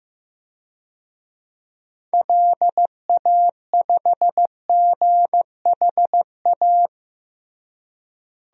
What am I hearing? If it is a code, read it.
Morse code, heard as a keyed tone, about 15 wpm, 700 Hz. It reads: LA5GHA
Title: DXCC-100 Call Signs - 6 Characters - Repeated 3x - 15wpm